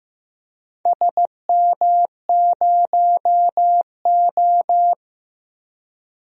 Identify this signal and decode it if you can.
Morse code, heard as a keyed tone, about 15 wpm, 700 Hz. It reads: SM0O
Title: DXCC-100 Call Signs - 4 Characters - 15wpm